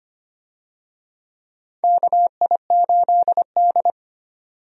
Morse code, heard as a keyed tone, about 25 wpm, 700 Hz. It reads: KI8D
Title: USA Call Signs - 4 Characters - 25wpm